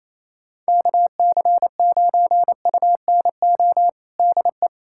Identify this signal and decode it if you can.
Morse code, heard as a keyed tone, about 28 wpm, 700 Hz. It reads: KC9UNO DE
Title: USA Callsign DE Callsign 28wpm